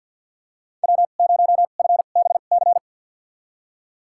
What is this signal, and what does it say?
Morse code, heard as a keyed tone, about 46 wpm, spaced as for 34 wpm, 700 Hz. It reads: W0FBC
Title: USA Call Signs - 2x Character Spacing - 50wpm